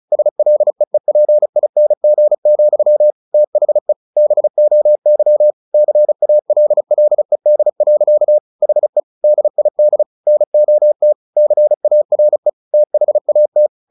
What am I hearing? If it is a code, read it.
Morse code, heard as a keyed tone, about 34 wpm, 600 Hz. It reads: SLEEPING, THE BOY CALLED. HE DID NOT CARE THAT
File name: the old man and the sea 2 35wpm